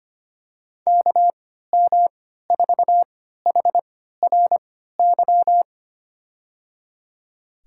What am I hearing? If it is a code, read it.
Morse code, heard as a keyed tone, about 25 wpm, spaced as for 14 wpm, 700 Hz. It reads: KM4HRY
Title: USA Call Signs - 6 Characters - 3x Character Spacing - 25wpm